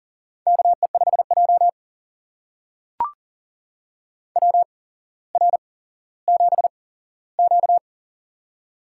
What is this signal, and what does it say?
Morse code, heard as a keyed tone, about 38 wpm, spaced as for 17 wpm, 700 Hz. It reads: KE5J WR7Q
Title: USA Call Signs - 4 Characters - 8x Character Spacing - 40wpm